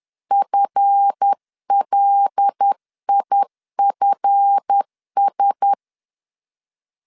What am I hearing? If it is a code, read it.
Morse code, heard as a keyed tone, about 10 wpm, 800 Hz. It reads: FLIFS
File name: LEZIONI / 7 GF9Z / 7 GF 10wpm 2514